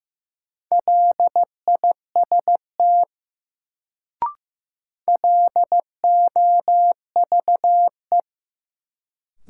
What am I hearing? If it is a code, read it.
Morse code, heard as a keyed tone, about 15 wpm, 700 Hz. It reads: LIST LOVE
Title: Sentences from Top 400 Words - New Word Drill - 15wpm